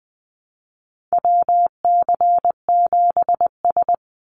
Morse code, heard as a keyed tone, about 20 wpm, 700 Hz. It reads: WC7S
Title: Call Signs - 20wpm